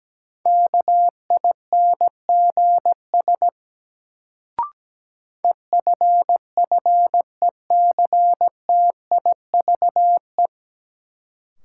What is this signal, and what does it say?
Morse code, heard as a keyed tone, about 17 wpm, 700 Hz. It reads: KINGS EFFECTIVE
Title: Sentences from Top 1000 Words - Review Part 6 - New Word Drill - 17wpm